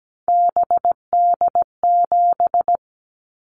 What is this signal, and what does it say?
Morse code, heard as a keyed tone, about 17 wpm, 700 Hz. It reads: BD7
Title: Callsign Prefixes for Top 100 DXCC - 17wpm